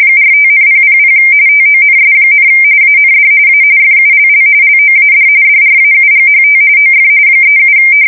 RTTY, about 45.45 baud. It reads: 45,000 TONS, AND A FULL LOAD DRAUGHT OF 37
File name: fsk170 / USS-New-Jersey